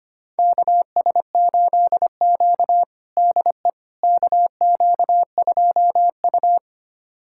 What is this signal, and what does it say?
Morse code, heard as a keyed tone, about 25 wpm, 700 Hz. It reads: KS8Q DE KQ2U
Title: USA Callsign DE Callsign - 4 Characters - 25wpm